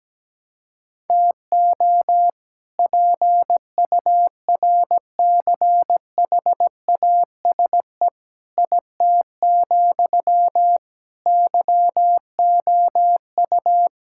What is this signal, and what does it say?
Morse code, heard as a keyed tone, about 17 wpm, 700 Hz. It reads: TO PURCHASE IT, YOU
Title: Sentences from Top 1000 Words - Review Part 5 - 17wpm